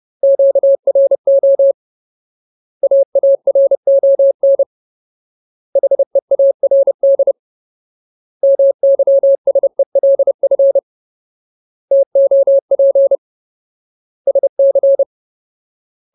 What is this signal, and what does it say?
Morse code, heard as a keyed tone, about 30 wpm, 550 Hz. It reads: QRO AARON HEARD MYSELF TOP SC
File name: Ultimate Word List / 30WPM Ultimate